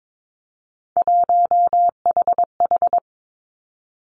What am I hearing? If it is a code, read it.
Morse code, heard as a keyed tone, about 22 wpm, 700 Hz. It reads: 1HH
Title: Three Challenging Letter-Number Combinations 22wpm